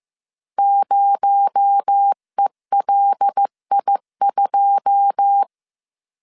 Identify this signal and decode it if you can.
Morse code, heard as a keyed tone, about 15 wpm, 800 Hz. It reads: 0ELI2
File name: CW / DET / 1 9 1500C S C 14wpm 3245